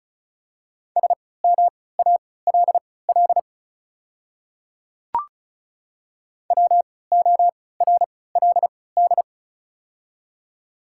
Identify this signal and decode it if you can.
Morse code, heard as a keyed tone, about 36 wpm, spaced as for 20 wpm, 700 Hz. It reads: SMALL WORLD
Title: 3x Character Spacing - Top 200 Words - 35wpm